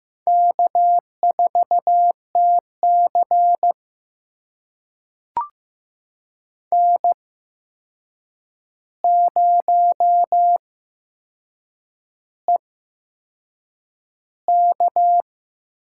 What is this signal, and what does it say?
Morse code, heard as a keyed tone, about 15 wpm, spaced as for 6 wpm, 700 Hz. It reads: K4TC N0EK
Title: USA Call Signs - 4 Characters - 8x Character Spacing - 15wpm